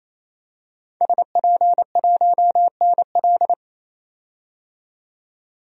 Morse code, heard as a keyed tone, about 27 wpm, 700 Hz. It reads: SP1NL
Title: DXCC-100 Call Signs - 5 Characters - 28wpm